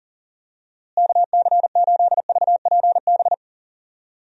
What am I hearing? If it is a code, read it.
Morse code, heard as a keyed tone, about 40 wpm, 700 Hz. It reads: KC8VPB
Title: USA Call Signs - 6 Characters - Repeated 2x - 40wpm